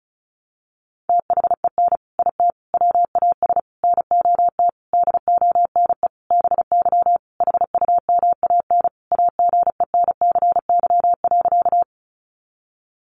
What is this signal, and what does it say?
Morse code, heard as a keyed tone, about 34 wpm, 700 Hz. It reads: THEN IT WAS NOT DONE BY HUMAN AGENCY.
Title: Sentences from Top 1300 Words 35wpm